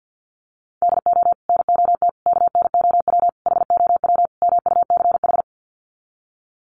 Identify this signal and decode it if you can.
Morse code, heard as a keyed tone, about 50 wpm, 700 Hz. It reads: DO NOT KNOW HOW MUCH
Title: Sets of 5 Words Encore 50wpm